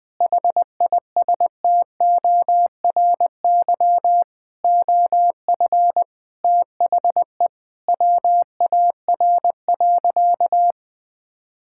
Morse code, heard as a keyed tone, about 21 wpm, 700 Hz. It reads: HISTORY OF THE WAR.
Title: Sentences with 4-Word Phrases - 20wpm